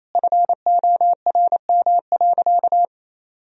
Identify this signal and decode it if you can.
Morse code, heard as a keyed tone, about 28 wpm, 700 Hz. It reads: FORM.
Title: Sentences from Top 1000 Words - Review Part 1 - 28wpm